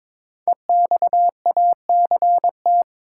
Morse code, heard as a keyed tone, about 22 wpm, 700 Hz. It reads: EXACT
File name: 5 Letter Words - First 3 Letters Given - 22wpm